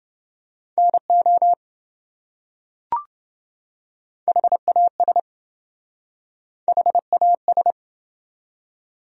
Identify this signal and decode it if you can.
Morse code, heard as a keyed tone, about 26 wpm, 700 Hz. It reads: NO HAS HAS